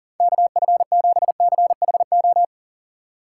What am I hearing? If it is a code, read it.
Morse code, heard as a keyed tone, about 40 wpm, 700 Hz. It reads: KF7CHO
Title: USA Call Signs - 6 Characters - 4x Character Spacing - 40wpm